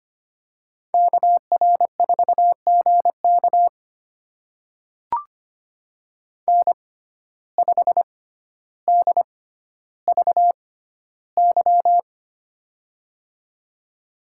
Morse code, heard as a keyed tone, about 25 wpm, spaced as for 13 wpm, 700 Hz. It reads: KR4GK N5DVY
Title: USA Call Signs - 5 Characters - 6x Character Spacing - 25wpm